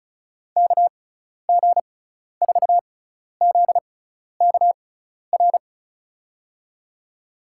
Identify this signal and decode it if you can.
Morse code, heard as a keyed tone, about 36 wpm, spaced as for 12 wpm, 690 Hz. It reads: KG4ZKR
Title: USA Call Signs - 6 Characters - 6x Character Spacing - 35wpm